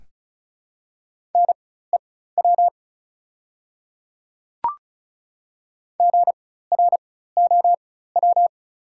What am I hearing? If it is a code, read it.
Morse code, heard as a keyed tone, about 34 wpm, spaced as for 16 wpm, 700 Hz. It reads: NEW GROW
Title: 4x Character Spacing - Top 300 Words - 35wpm